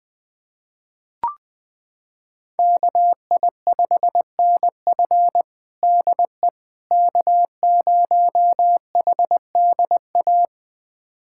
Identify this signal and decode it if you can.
Morse code, heard as a keyed tone, about 20 wpm, 700 Hz. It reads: KI5NF DE K0HDA
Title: USA Callsign DE Callsign - 5 Characters - 20wpm